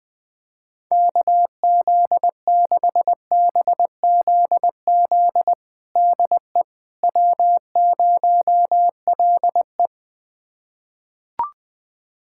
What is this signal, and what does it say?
Morse code, heard as a keyed tone, about 20 wpm, 700 Hz. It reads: KZ6BZZ DE W0LE
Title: USA Callsign DE Callsign 20wpm